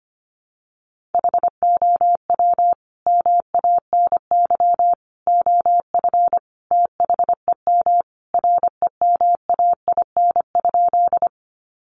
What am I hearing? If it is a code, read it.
Morse code, heard as a keyed tone, about 25 wpm, 700 Hz. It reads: HOW MANY OF THEM REMAIN?
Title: Sentences from Top 1200 Words 25wpm